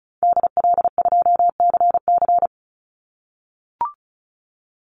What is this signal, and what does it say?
Morse code, heard as a keyed tone, about 34 wpm, 700 Hz. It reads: DL2CC
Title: Call Signs - 35wpm